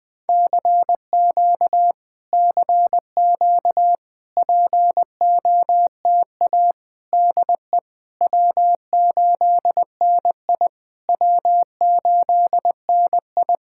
Morse code, heard as a keyed tone, about 20 wpm, 700 Hz. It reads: CQ CQ POTA DE W8NI W8NI
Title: POTA QSO Practice - 20wpm